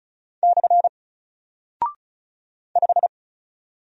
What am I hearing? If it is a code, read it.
Morse code, heard as a keyed tone, about 36 wpm, 700 Hz. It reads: / 5
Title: Single Character (Letter:Number:Punctuation:Prosign) 35wpm